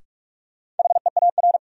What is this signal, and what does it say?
Morse code, heard as a keyed tone, about 42 wpm, 700 Hz. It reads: HEAR
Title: Sentences from Top 300 Words - New Word Drill - 45wpm